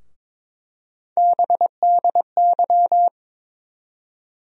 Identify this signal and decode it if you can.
Morse code, heard as a keyed tone, about 22 wpm, 700 Hz. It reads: BDY
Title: USA Call Sign Postfixes - 3 Letters 22wpm